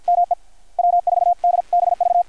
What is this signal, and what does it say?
Morse code, heard as a keyed tone, about 53 wpm, 700 Hz, with radio noise. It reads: ME W4DBV
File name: archived / CRSnet